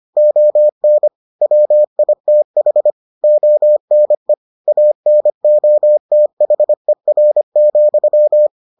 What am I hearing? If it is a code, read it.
Morse code, heard as a keyed tone, about 25 wpm, 600 Hz. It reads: ON WITH ONE ANOTHER,